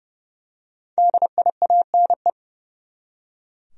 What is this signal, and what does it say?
Morse code, heard as a keyed tone, about 30 wpm, 700 Hz. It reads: DIANE
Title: USA Common Names - Max 5 Letters - 30wpm